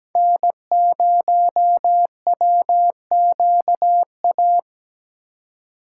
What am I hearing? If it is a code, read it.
Morse code, heard as a keyed tone, about 20 wpm, 700 Hz. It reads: N0WQA
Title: Speed Racing - USA 1x3 Call Signs 17wpm